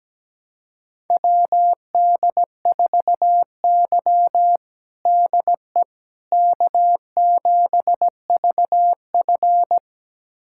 Morse code, heard as a keyed tone, about 17 wpm, 700 Hz. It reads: WD4Y DE K7VF
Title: USA Callsign DE Callsign - 4 Characters - 17wpm